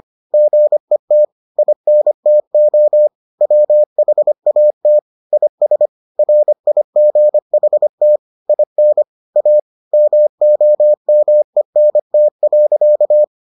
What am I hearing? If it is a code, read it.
Morse code, heard as a keyed tone, about 25 wpm, 600 Hz. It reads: GET INTO WHAT IS RIGHT IN A MOMENT.